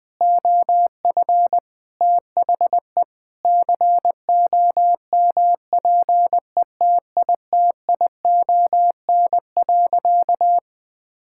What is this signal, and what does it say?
Morse code, heard as a keyed tone, about 21 wpm, 700 Hz. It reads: OF THE COMPETITION.
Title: Sentences with 6-Word Phrases - 20wpm